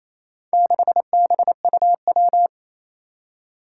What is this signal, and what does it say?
Morse code, heard as a keyed tone, about 27 wpm, 700 Hz. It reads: 6BUW
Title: Four Challenging Letter-Number Combinations 28wpm